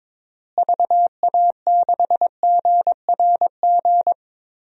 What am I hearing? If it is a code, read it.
Morse code, heard as a keyed tone, about 21 wpm, 700 Hz. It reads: VA6GRG
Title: DXCC-100 Call Signs - 6 Characters - Repeated 3x - 22wpm